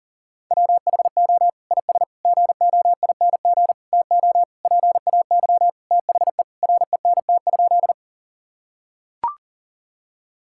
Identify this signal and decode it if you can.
Morse code, heard as a keyed tone, about 40 wpm, 700 Hz. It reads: WHO IS GOING TO PAY THE RENT?
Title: Sentences from Top 1000 Words - Review Part 5 - 40wpm